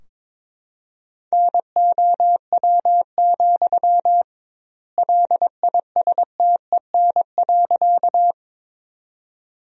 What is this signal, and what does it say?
Morse code, heard as a keyed tone, about 22 wpm, 700 Hz. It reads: NOW, LISTEN.
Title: 2x Word-Spacing - Sentences from Top 400 Words - 22wpm